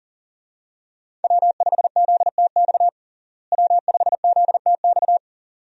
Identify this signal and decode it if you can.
Morse code, heard as a keyed tone, about 37 wpm, 700 Hz. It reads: W5ZTX W5ZTX
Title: USA Call Signs - 5 Characters - Repeated 2x - 40wpm